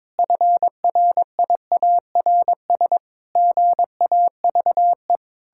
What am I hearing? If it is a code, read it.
Morse code, heard as a keyed tone, about 22 wpm, 700 Hz. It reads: FRIARS GAVE
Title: Pangram Challenge v2 - 22wpm